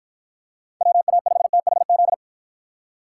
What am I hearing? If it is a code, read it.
Morse code, heard as a keyed tone, about 52 wpm, 700 Hz. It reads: WA5THZ